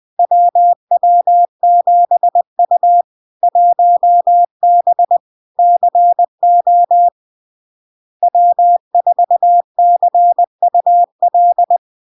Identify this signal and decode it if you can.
Morse code, heard as a keyed tone, about 20 wpm, 700 Hz. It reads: WW7U 1B CO W4CUL
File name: LARK-FD-2022-Log-in-CW-20wpm-700-Hz